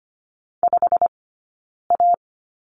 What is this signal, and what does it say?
Morse code, heard as a keyed tone, about 25 wpm, spaced as for 9 wpm, 690 Hz. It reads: 5A
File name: Lesson 15d - Number 5 - Two Characters 25wpm